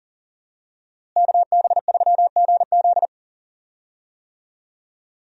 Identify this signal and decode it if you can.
Morse code, heard as a keyed tone, about 40 wpm, 700 Hz. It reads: KB3GZ
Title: USA Call Signs - 5 Characters - Repeated 2x - 40wpm